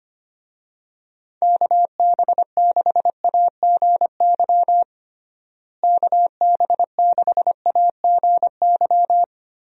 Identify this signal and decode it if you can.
Morse code, heard as a keyed tone, about 25 wpm, 700 Hz. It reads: KB6AGY KB6AGY
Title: USA Call Signs - 6 Characters - Repeated 3x - 25wpm